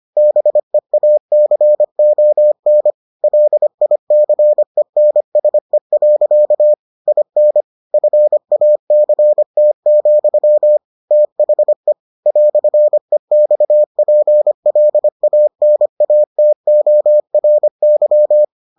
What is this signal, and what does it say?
Morse code, heard as a keyed tone, about 25 wpm, 600 Hz. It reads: BEACON LICENSE. IN FACT, THE "EXPLANATORY